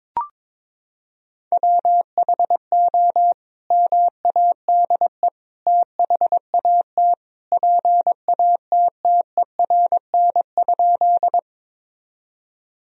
Morse code, heard as a keyed tone, about 22 wpm, 700 Hz. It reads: WHO MADE THAT PATTERN?
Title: Sentences from Top 1000 Words - Review Part 2 - 22wpm